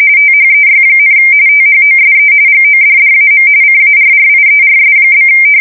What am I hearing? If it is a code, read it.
RTTY, about 45.45 baud. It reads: THEY, THOSE GODDAM CARRIERS?
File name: fsk170 / USS-New-Jersey